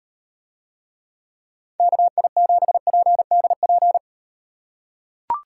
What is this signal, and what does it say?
Morse code, heard as a keyed tone, about 38 wpm, 700 Hz. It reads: KI7PDP